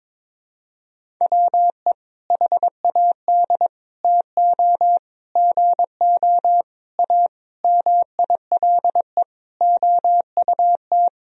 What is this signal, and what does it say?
Morse code, heard as a keyed tone, about 22 wpm, 700 Hz. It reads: WE HAD TO GO A MILE OUT
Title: Sentences from Top 1000 Words - Review Part 1 - 22wpm